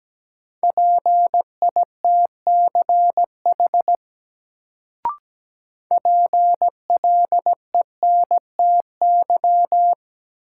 Morse code, heard as a keyed tone, about 17 wpm, 700 Hz. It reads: PITCH PLENTY
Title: Sentences from Top 1300 Words - New Word Drill Alt - 17wpm